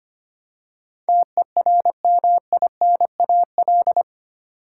Morse code, heard as a keyed tone, about 25 wpm, 700 Hz. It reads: TERMINAL
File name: 8 Letter Words - First 3 Letters Given - 25wpm